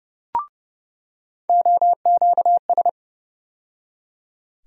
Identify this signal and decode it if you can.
Morse code, heard as a keyed tone, about 31 wpm, 700 Hz. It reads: OQS
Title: USA Call Sign Postfixes - 3 Letters 30wpm